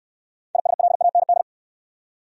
Morse code, heard as a keyed tone, about 69 wpm, 690 Hz. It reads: ISLAND